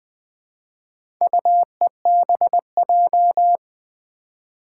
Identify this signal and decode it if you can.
Morse code, heard as a keyed tone, about 20 wpm, 700 Hz. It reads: UEBJ
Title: Four Challenging Letter-Number Combinations 20wpm